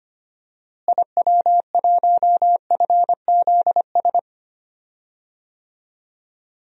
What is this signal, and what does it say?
Morse code, heard as a keyed tone, about 25 wpm, 700 Hz. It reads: IW1FZS
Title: DXCC-100 Call Signs - 6 Characters - Repeated 2x - 25wpm